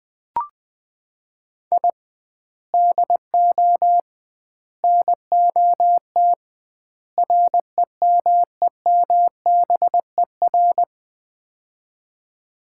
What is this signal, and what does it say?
Morse code, heard as a keyed tone, about 21 wpm, 700 Hz. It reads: I DO NOT REMEMBER
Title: 2x Word-Spacing - Sets of 4 Words - 20wpm